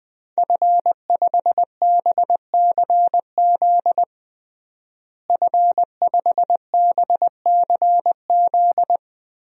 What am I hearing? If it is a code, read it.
Morse code, heard as a keyed tone, about 21 wpm, 700 Hz. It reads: F5BCZ F5BCZ
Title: DXCC-100 Call Signs - 5 Characters - Repeated 3x - 20wpm